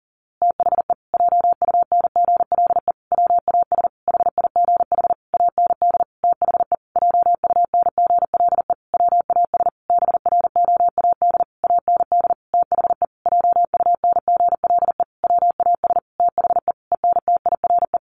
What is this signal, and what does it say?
Morse code, heard as a keyed tone, about 40 wpm, 700 Hz. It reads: THE JUNGLE WAS HIGH AND THE JUNGLE WAS BROAD AND THE JUNGLE WAS THE ENTIRE
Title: A Sound Of Thunder 40wpm